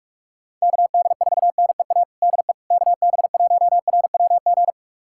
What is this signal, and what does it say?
Morse code, heard as a keyed tone, about 46 wpm, 700 Hz. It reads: KD4NEA DE KB1RWG
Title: USA Callsign DE Callsign - 6 Characters - 45wpm